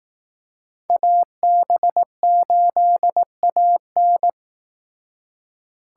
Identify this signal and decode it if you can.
Morse code, heard as a keyed tone, about 19 wpm, 700 Hz. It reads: AB8AN
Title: Speed Racing - USA 2x2 Call Signs 15wpm